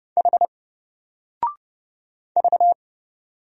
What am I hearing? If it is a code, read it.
Morse code, heard as a keyed tone, about 29 wpm, 700 Hz. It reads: H V
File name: Single-Letter-Number-30wpm